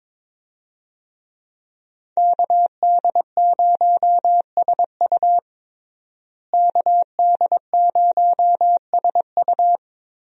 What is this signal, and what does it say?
Morse code, heard as a keyed tone, about 22 wpm, 700 Hz. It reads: KD0SU KD0SU
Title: USA Call Signs - 5 Characters - Repeated 2x - 22wpm